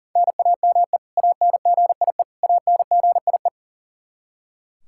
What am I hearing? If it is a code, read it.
Morse code, heard as a keyed tone, about 37 wpm, 700 Hz. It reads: NAME ANGIE ANGIE
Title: Practice QSO - Exchange 1 - Beginning - 40wpm